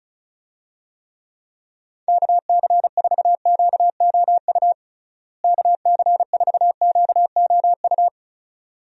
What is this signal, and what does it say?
Morse code, heard as a keyed tone, about 34 wpm, 700 Hz. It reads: KC4QOU KC4QOU
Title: USA Call Signs - 6 Characters - Repeated 2x - 35wpm